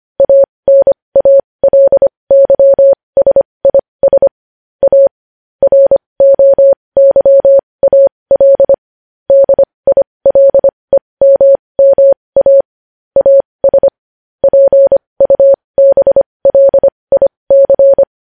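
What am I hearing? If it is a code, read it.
Morse code, heard as a keyed tone, about 25 wpm, spaced as for 20 wpm, 550 Hz. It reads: ANALYSIS A ROYAL DILEMMA AS PUBLIC
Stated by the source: Morse code news headlines.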